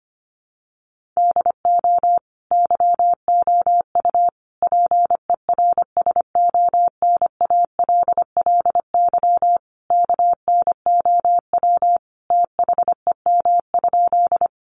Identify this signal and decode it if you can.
Morse code, heard as a keyed tone, about 25 wpm, 700 Hz. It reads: DO YOU PERSONALLY KNOW THEM?
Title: Sentences from Top 1400 Words 25wpm